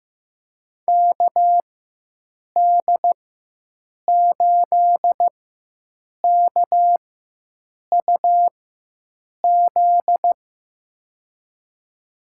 Morse code, heard as a keyed tone, about 15 wpm, spaced as for 7 wpm, 700 Hz. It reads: KD8KUZ
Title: USA Call Signs - 6 Characters - 4x Character Spacing - 15wpm